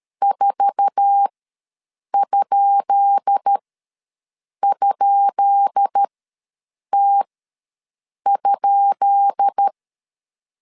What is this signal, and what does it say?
Morse code, heard as a keyed tone, about 13 wpm, spaced as for 7 wpm, 800 Hz. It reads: 4??T?
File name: LEZIONI / 10 SIGNS / 10B INT ws 12wpm 3527